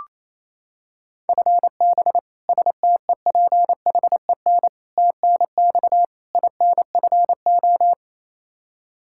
Morse code, heard as a keyed tone, about 28 wpm, 700 Hz. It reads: FB STEPHEN TNX INFO
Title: Practice QSO - Exchange 3 - More Info - 28wpm